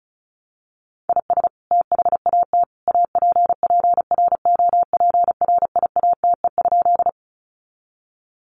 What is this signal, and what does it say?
Morse code, heard as a keyed tone, about 34 wpm, 700 Hz. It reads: IS THAT APPROPRIATE?
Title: Sentences from Top 900 Words 35wpm